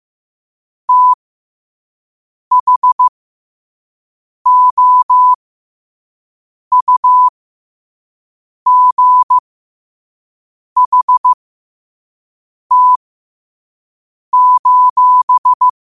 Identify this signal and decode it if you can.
Morse code, heard as a keyed tone, about 15 wpm, spaced as for 5 wpm, 1000 Hz. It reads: THOUGHT:
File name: qotdfiles / 5wpm